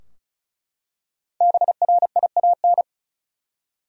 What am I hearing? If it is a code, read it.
Morse code, heard as a keyed tone, about 36 wpm, 700 Hz. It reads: BRIAN
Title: USA Common Names - Max 5 Letters - 35wpm